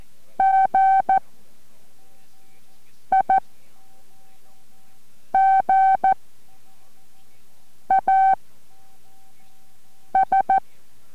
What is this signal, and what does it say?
Morse code, heard as a keyed tone, about 14 wpm, 770 Hz. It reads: G I G A S